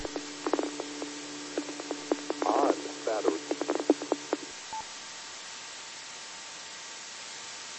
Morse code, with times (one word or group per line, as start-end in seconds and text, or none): none